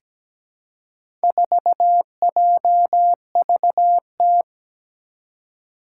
Morse code, heard as a keyed tone, about 17 wpm, 700 Hz. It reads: 4JVT
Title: Four Challenging Letter-Number Combinations 17wpm